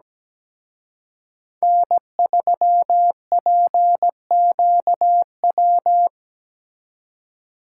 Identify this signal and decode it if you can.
Morse code, heard as a keyed tone, about 20 wpm, 700 Hz. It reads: N3PQW
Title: Speed Racing - USA 1x3 Call Signs 17wpm